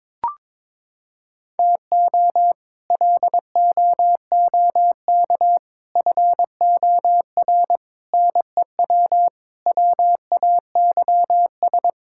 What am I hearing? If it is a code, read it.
Morse code, heard as a keyed tone, about 21 wpm, 700 Hz. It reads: TO LOOK FOR NEW WAYS
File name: practice / 22wpm / Sets of 6 Words - 22wpm